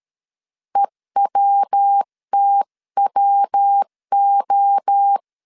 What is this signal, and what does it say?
Morse code, heard as a keyed tone, about 13 wpm, 800 Hz. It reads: EWTWO
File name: CW / DET / 1 3 1500C S A 12wpm 2905